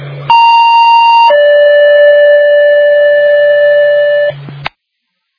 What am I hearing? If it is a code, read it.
Two-tone page, heard as a one-second tone, then a three-second tone, about 930 then 580 Hz, with noise. It is heard from 0.3 to 4.3 s.